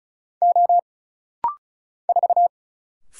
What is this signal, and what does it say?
Morse code, heard as a keyed tone, about 33 wpm, 700 Hz. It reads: O 4
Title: Single Letter-Number - Rapid-Fire 35wpm